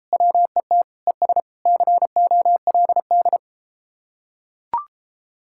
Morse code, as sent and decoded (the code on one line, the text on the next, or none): .-- . - / . ... / -.-. --- .-.. -..
WET ES COLD